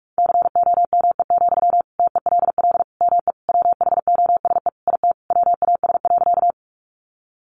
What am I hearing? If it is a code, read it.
Morse code, heard as a keyed tone, about 44 wpm, 700 Hz. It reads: COME, TELL ME WHOSE IT WAS.
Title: Sentences from Top 1200 Words 45wpm